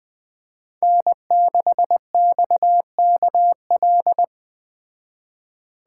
Morse code, heard as a keyed tone, about 20 wpm, 700 Hz. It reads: N6XKL